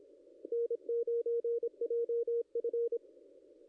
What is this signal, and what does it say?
Morse code, heard as a keyed tone, about 26 wpm, 440 Hz. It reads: N9JF
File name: contests / ss-cw-k5tr-op-n5rz -0100